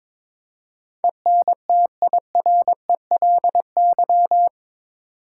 Morse code, heard as a keyed tone, about 21 wpm, 700 Hz. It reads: ENTIRELY
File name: practice / 22wpm / Eight Letter Words - 22wpm